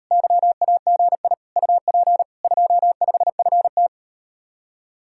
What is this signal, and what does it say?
Morse code, heard as a keyed tone, about 38 wpm, 700 Hz. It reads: YAGI UP 25FT